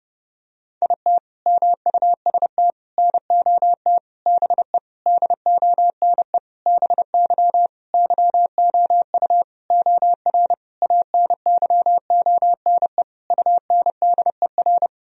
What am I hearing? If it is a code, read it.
Morse code, heard as a keyed tone, about 30 wpm, 700 Hz. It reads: IT MUST NOT BE DONE BY YOU OR ANYONE UNDER
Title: Sentences from Top 1000 Words - Review Part 1 - 30wpm